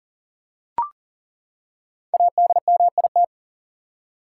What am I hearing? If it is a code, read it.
Morse code, heard as a keyed tone, about 37 wpm, 700 Hz. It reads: ADMIT
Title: Five Letter Words - 40wpm